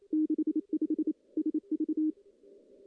Morse code, heard as a keyed tone, about 28 wpm, 310 Hz. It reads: 65 SV
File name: contests / CW-SS-k5tr- -1600